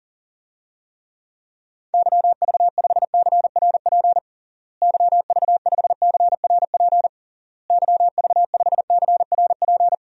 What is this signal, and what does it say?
Morse code, heard as a keyed tone, about 40 wpm, 700 Hz. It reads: YV5CRP YV5CRP YV5CRP
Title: DXCC-100 Call Signs - 6 Characters - Repeated 3x - 40wpm